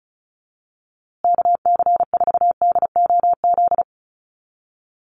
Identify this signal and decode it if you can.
Morse code, heard as a keyed tone, about 34 wpm, 700 Hz. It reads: KC4DOZ
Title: USA Call Signs Repeated 3x 35wpm